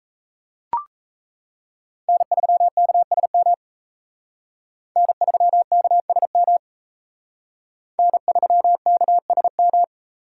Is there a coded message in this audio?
Yes: Morse code from 2.1 to 9.8 s, heard as a keyed tone, about 35 wpm, 700 Hz. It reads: N3KSM N3KSM N3KSM